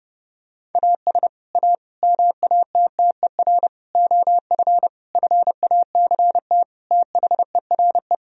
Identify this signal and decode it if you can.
Morse code, heard as a keyed tone, about 29 wpm, 700 Hz. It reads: AS A MATTER OF FACT THERE
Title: Sentences from Top 1000 Words - Review Part 3 - 30wpm